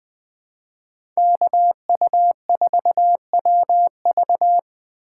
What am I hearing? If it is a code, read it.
Morse code, heard as a keyed tone, about 20 wpm, 700 Hz. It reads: KU4WV
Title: USA Call Signs - 5 Characters - 4x Character Spacing - 20wpm